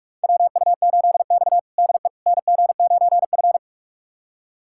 Morse code, heard as a keyed tone, about 43 wpm, 700 Hz. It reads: WU8X DE NG9F
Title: USA Callsign DE Callsign - 4 Characters - 45wpm